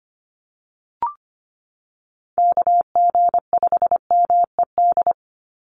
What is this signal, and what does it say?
Morse code, heard as a keyed tone, about 25 wpm, 700 Hz. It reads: KG5MED
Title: USA Call Signs 25wpm